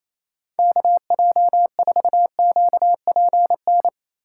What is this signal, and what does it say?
Morse code, heard as a keyed tone, about 34 wpm, 700 Hz. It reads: KJ4QPN
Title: Speed Racing - USA 2x3 Call Signs 28wpm